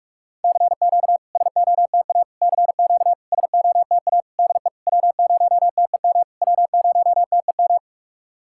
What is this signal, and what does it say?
Morse code, heard as a keyed tone, about 44 wpm, 690 Hz. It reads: CQ SOTA CQ SOTA DE W0TEM W0TEM
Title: SOTA QSO Practice 45wpm